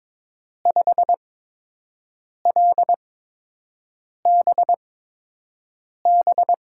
Morse code, heard as a keyed tone, about 23 wpm, spaced as for 6 wpm, 700 Hz. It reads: 5LBB
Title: USA Call Signs - 5 Characters - 8x Character Spacing - 22wpm